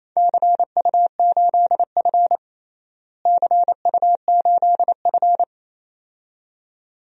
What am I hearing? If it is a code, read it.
Morse code, heard as a keyed tone, about 28 wpm, 700 Hz. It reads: CU8F CU8F
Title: DXCC-100 Call Signs - 4 Characters - Repeated 2x - 28wpm